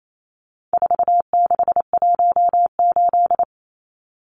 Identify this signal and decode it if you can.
Morse code, heard as a keyed tone, about 28 wpm, 700 Hz. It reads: 4618
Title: Numbers - 4-Digit 28wpm